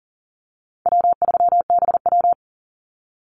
Morse code, heard as a keyed tone, about 40 wpm, 700 Hz. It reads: W3BW
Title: Easy Call Signs - 40wpm